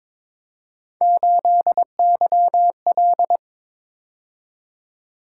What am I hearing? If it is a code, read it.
Morse code, heard as a keyed tone, about 26 wpm, 700 Hz. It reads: 8YL
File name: Speed-Racing - Three Letter-Number Combinations - 22wpm